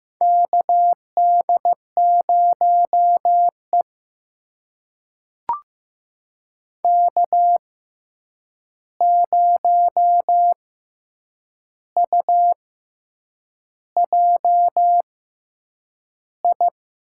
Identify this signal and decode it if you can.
Morse code, heard as a keyed tone, about 15 wpm, spaced as for 7 wpm, 700 Hz. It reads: KD0E K0UJI
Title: USA Call Signs - 6x Character Spacing - 15wpm